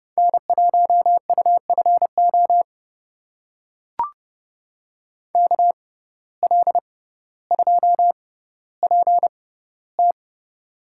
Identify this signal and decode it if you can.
Morse code, heard as a keyed tone, about 31 wpm, spaced as for 16 wpm, 700 Hz. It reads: N1UFO KL2PT
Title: USA Call Signs - 5 Characters - 6x Character Spacing - 30wpm